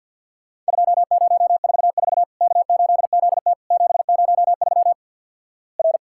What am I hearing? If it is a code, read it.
Morse code, heard as a keyed tone, about 52 wpm, 700 Hz. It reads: 2044 K8ZT 703 R
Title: Worked All Europe DX Contest - QTC Exchange with Repeats - 50wpm